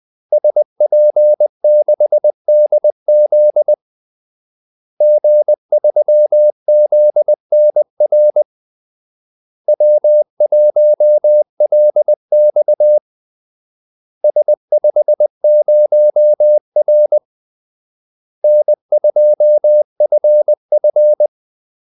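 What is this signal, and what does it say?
Morse code, heard as a keyed tone, about 20 wpm, 600 Hz. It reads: SP6DZ G3ZNR W1LX S50R N2FF
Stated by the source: Morse call-sign practice at 20 wpm.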